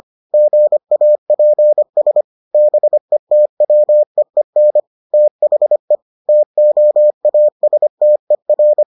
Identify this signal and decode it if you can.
Morse code, heard as a keyed tone, about 25 wpm, 600 Hz. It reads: GAPS BETWEEN THE TOASTER